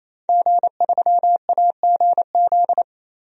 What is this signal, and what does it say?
Morse code, heard as a keyed tone, about 28 wpm, 700 Hz. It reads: G3AGZ
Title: DXCC-100 Call Signs - 5 Characters - 28wpm